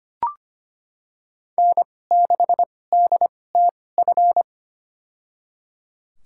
Morse code, heard as a keyed tone, about 25 wpm, spaced as for 18 wpm, 700 Hz. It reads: N6DTF
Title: USA Call Signs - 2x Character Spacing - 25wpm